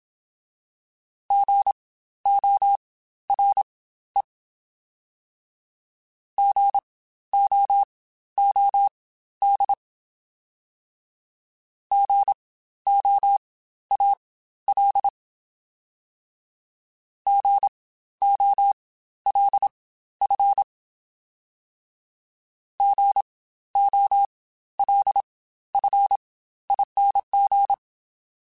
Morse code, heard as a keyed tone, about 27 wpm, spaced as for 13 wpm, 800 Hz. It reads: GORE GOOD GOAL GOLF GOLFING
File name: Spaced Alphas / Spaced Alphas mp3s / Gs sp @25